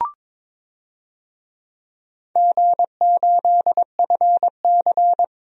Morse code, heard as a keyed tone, about 22 wpm, 700 Hz. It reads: G8FC
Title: DXCC-100 Call Signs - 4 Characters - 22wpm